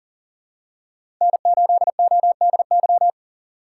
Morse code, heard as a keyed tone, about 38 wpm, 700 Hz. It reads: N8ODY
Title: USA Call Signs - 5 Characters - 3x Character Spacing - 40wpm